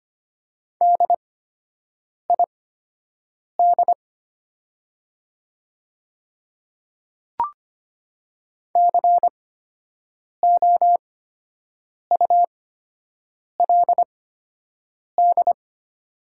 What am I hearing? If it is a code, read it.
Morse code, heard as a keyed tone, about 25 wpm, spaced as for 7 wpm, 700 Hz. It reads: DID COULD
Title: 8x Character Spacing - Top 200 Words - 25wpm